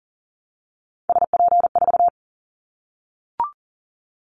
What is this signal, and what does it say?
Morse code, heard as a keyed tone, about 37 wpm, 700 Hz. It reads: SP4
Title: Callsign Prefixes for Top 25 DXCC - 40wpm